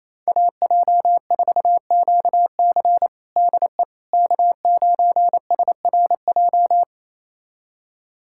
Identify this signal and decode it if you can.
Morse code, heard as a keyed tone, about 28 wpm, 700 Hz. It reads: AJ4QC DE K9SRJ
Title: USA Callsign DE Callsign - 5 Characters - 28wpm